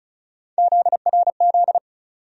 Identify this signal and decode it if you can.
Morse code, heard as a keyed tone, about 34 wpm, 700 Hz. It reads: ZRZ
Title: USA Call Sign Postfixes - 3 Letters 35wpm